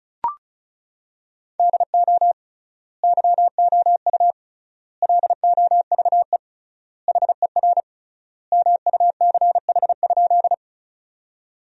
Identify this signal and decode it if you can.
Morse code, heard as a keyed tone, about 36 wpm, 700 Hz. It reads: DO YOU LOVE HER MUCH?
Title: 3x Word-Spacing - Sentences from Top 400 Words - 35wpm